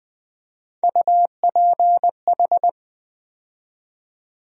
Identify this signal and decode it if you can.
Morse code, heard as a keyed tone, about 20 wpm, 700 Hz. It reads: UPH
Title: USA Call Sign Postfixes - 3 Letters 20wpm